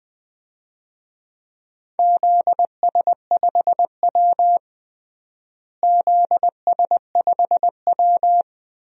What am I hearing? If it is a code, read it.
Morse code, heard as a keyed tone, about 20 wpm, 700 Hz. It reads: ZS5W ZS5W
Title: DXCC-100 Call Signs - 4 Characters - Repeated 2x - 20wpm